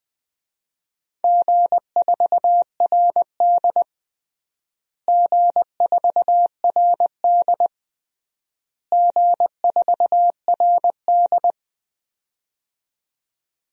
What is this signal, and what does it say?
Morse code, heard as a keyed tone, about 21 wpm, 700 Hz. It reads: G4RD G4RD G4RD
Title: DXCC-100 Call Signs - 4 Characters - Repeated 3x - 20wpm